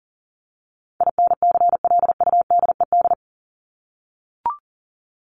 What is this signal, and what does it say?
Morse code, heard as a keyed tone, about 40 wpm, 700 Hz. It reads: INCLUDED
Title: ED Suffix - Top 100 Words with Base Words 40wpm